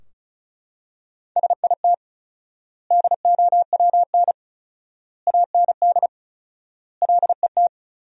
Morse code, heard as a keyed tone, about 36 wpm, 700 Hz. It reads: SIT DOWN AND LET
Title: 4x Word-Spacing - Sentences from Top 500 Words - 35wpm